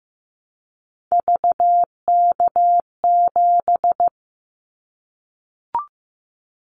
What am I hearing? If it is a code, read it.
Morse code, heard as a keyed tone, about 15 wpm, 700 Hz. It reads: VK7
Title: Callsign Prefixes for Top 50 DXCC - 15wpm